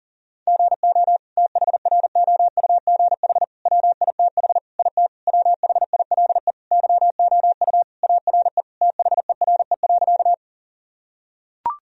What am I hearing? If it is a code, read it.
Morse code, heard as a keyed tone, about 40 wpm, 700 Hz. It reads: GO THROUGH WITH IT WHILE YOU ARE THERE.
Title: Sentences from Top 1000 Words - Review Part 1 - 40wpm